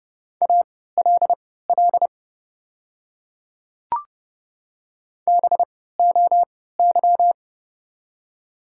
Morse code, heard as a keyed tone, about 30 wpm, spaced as for 17 wpm, 700 Hz. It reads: ALL BOY
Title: 3x Character Spacing - Top 300 Words - 30wpm